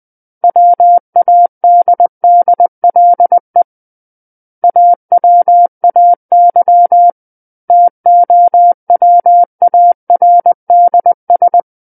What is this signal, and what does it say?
Morse code, heard as a keyed tone, about 20 wpm, 700 Hz. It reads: WADDLE AWAY TOWARDS
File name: 20 WPM